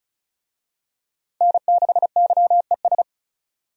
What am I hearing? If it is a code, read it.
Morse code, heard as a keyed tone, about 36 wpm, 700 Hz. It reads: N6YES
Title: USA Call Signs - 5 Characters - 35wpm